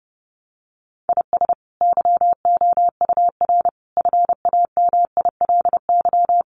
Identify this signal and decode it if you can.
Morse code, heard as a keyed tone, about 30 wpm, 700 Hz. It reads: IS YOUR FAMILY
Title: Sentences from Top 400 Words 30wpm